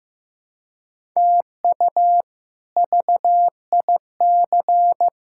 Morse code, heard as a keyed tone, about 15 wpm, 700 Hz. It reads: TU VIC
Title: K1USN SST QSO Practice - 15wpm